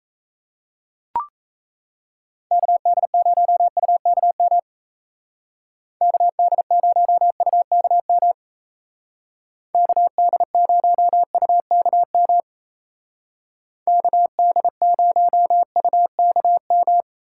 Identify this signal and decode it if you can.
Morse code, heard as a keyed tone, about 35 wpm, 700 Hz. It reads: KD0UKM KD0UKM KD0UKM KD0UKM
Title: Speed Racing - USA 2x3 Call Signs 28wpm